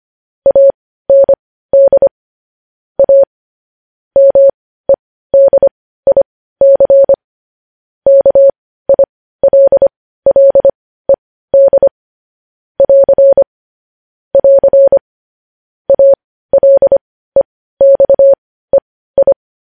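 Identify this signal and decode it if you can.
Morse code, heard as a keyed tone, about 25 wpm, spaced as for 15 wpm, 550 Hz. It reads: AND A MEDIC KILLED <AR> <AR> ALEXEI
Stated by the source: Morse code news headlines at 15 wpm.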